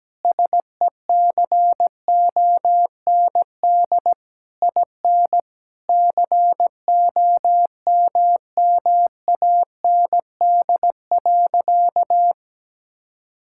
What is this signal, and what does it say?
Morse code, heard as a keyed tone, about 17 wpm, 700 Hz. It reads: SECOND IN COMMAND.
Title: Sentences from Top 1600 Words 17wpm